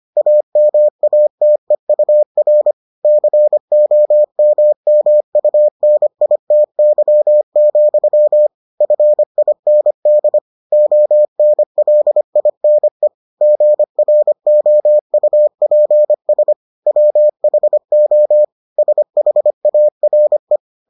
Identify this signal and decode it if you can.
Morse code, heard as a keyed tone, about 25 wpm, 600 Hz. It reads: AMATEUR COMMUNITY, FIND ONLINE GROUPS WHO SHARE